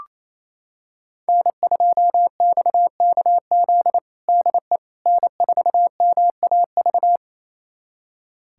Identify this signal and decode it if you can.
Morse code, heard as a keyed tone, about 28 wpm, 700 Hz. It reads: N2XKZ DE N4MAV
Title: USA Callsign DE Callsign - 5 Characters - 28wpm